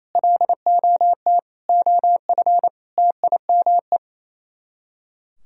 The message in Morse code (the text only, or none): LOT OF TIME